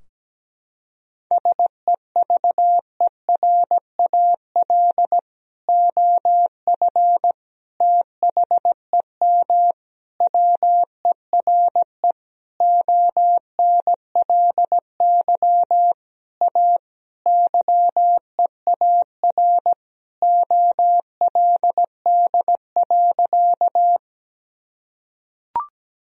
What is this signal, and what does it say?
Morse code, heard as a keyed tone, about 17 wpm, 700 Hz. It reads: SEVERAL OF THEM WERE ONLY A YEAR OLD.
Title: Sentences from Top 1000 Words - Review Part 2 - 17wpm